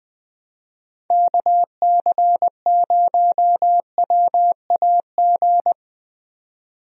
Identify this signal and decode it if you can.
Morse code, heard as a keyed tone, about 20 wpm, 700 Hz. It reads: KC0WAG
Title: USA Call Signs - 6 Characters - 20wpm